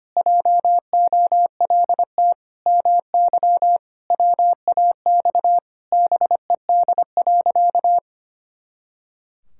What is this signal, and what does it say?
Morse code, heard as a keyed tone, about 25 wpm, 700 Hz. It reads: JOLT MY WAX BED.
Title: Pangram Challenge v2 - 25wpm